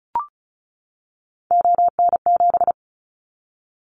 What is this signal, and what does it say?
Morse code, heard as a keyed tone, about 36 wpm, 700 Hz. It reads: ON7
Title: Callsign Prefixes for Top 50 DXCC - 35wpm